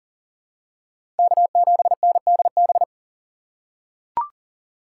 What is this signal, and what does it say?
Morse code, heard as a keyed tone, about 40 wpm, 700 Hz. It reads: K7NDB
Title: USA Call Signs - 5 Characters - 2x Character Spacing - 40wpm